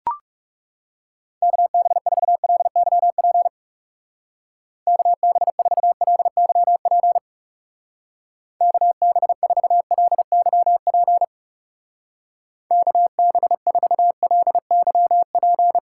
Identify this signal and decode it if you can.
Morse code, heard as a keyed tone, about 37 wpm, 700 Hz. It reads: KB4LYP KB4LYP KB4LYP KB4LYP
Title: Speed Racing - USA 2x3 Call Signs 30wpm